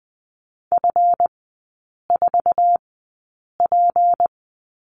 Morse code, heard as a keyed tone, about 20 wpm, spaced as for 8 wpm, 700 Hz. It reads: F4P
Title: 152 Lesson 25e - Letter p - Three Characters 20wpm